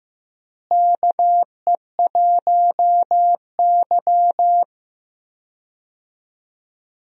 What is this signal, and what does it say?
Morse code, heard as a keyed tone, about 15 wpm, 700 Hz. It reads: KE1Y
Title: USA Call Signs - 4 Characters - Repeated 2x - 15wpm